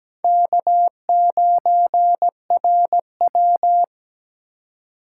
Morse code, heard as a keyed tone, about 20 wpm, 700 Hz. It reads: K9RW
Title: Speed Racing - USA 1x2 Call Signs 17wpm